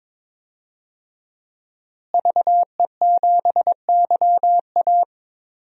Morse code, heard as a keyed tone, about 21 wpm, 700 Hz. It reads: VE7YA
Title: DXCC-100 Call Signs - 5 Characters - Repeated 2x - 22wpm